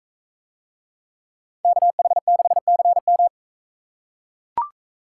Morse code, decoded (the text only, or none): KH6CM